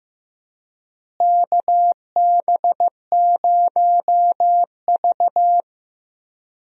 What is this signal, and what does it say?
Morse code, heard as a keyed tone, about 15 wpm, 700 Hz. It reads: KB0V
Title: USA Call Signs - 4 Characters - Repeated 3x - 15wpm